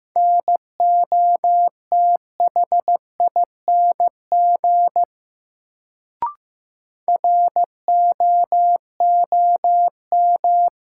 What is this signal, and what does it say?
Morse code, heard as a keyed tone, about 15 wpm, 700 Hz. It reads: NOTHING ROOM
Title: Sets of 4 Words - New Word Drill - 15wpm